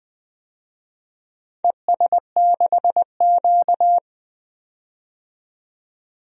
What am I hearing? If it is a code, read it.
Morse code, heard as a keyed tone, about 20 wpm, 700 Hz. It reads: ES6Q
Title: DXCC-100 Call Signs - 4 Characters - 20wpm